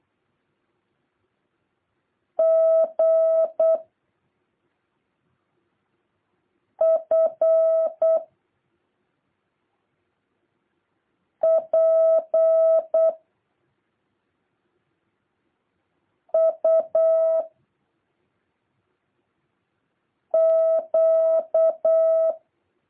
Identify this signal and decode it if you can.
Morse code, heard as a keyed tone, about 8 wpm, 650 Hz. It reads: G F P U Q
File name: jh8cbh / 2023cw